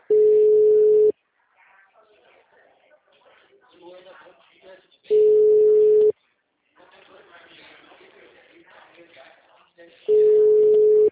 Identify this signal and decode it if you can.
Ringback tone (Europe, 425 Hz), with speech.